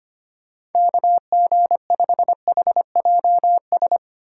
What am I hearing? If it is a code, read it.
Morse code, heard as a keyed tone, about 31 wpm, 700 Hz. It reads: KG5HJS